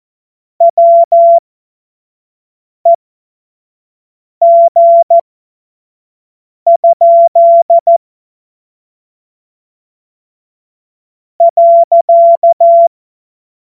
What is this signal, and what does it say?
Morse code, heard as a keyed tone, about 14 wpm, spaced as for 5 wpm, 680 Hz. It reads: WEG? .
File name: Morsekurs / Witze / Witz 27. Juni 2023 25 F70